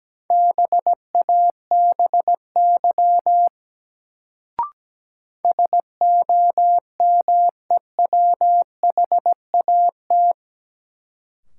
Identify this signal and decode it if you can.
Morse code, heard as a keyed tone, about 17 wpm, 700 Hz. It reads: BABY SOMEWHAT
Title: Sentences from Top 1300 Words - New Word Drill Alt - 17wpm